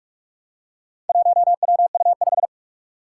Morse code, heard as a keyed tone, about 46 wpm, 700 Hz. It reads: 1WU5
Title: Four Challenging Letter-Number Combinations 45wpm